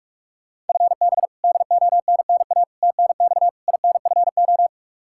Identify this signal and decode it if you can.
Morse code, heard as a keyed tone, about 44 wpm, 700 Hz. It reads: FB DONNA TNX INFO
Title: Practice QSO - Exchange 3 - More Info - 45wpm